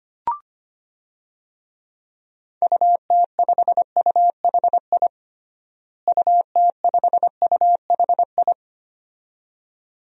Morse code, heard as a keyed tone, about 25 wpm, 700 Hz. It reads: UT5UHI UT5UHI
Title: DXCC-100 Call Signs - 6 Characters - Repeated 2x - 25wpm